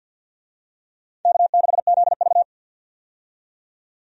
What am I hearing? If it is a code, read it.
Morse code, heard as a keyed tone, about 52 wpm, 700 Hz. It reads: K6ZV
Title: USA Call Signs - 4 Characters - 50wpm